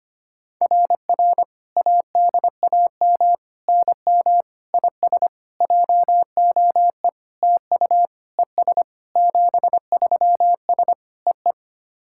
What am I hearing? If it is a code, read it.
Morse code, heard as a keyed tone, about 25 wpm, 690 Hz. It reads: RR ADAM NM IS JOE TU ES 73S EE
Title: SOTA QSO Practice 25wpm